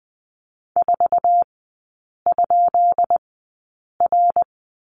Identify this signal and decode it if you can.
Morse code, heard as a keyed tone, about 20 wpm, spaced as for 8 wpm, 700 Hz. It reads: 4?R
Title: 133 Lesson 22e - Question Mark - Three Characters 20wpm